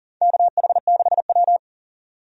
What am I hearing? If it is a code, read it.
Morse code, heard as a keyed tone, about 40 wpm, 700 Hz. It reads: KH6W
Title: DXCC-100 Call Signs - 4 Characters - 40wpm